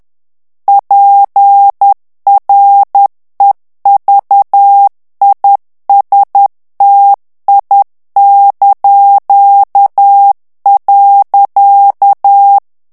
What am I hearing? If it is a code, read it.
Morse code, heard as a keyed tone, about 10 wpm, 800 Hz. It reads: PREVISTI).
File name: ITALIEN texte clair 2 vitesse 10 mots minute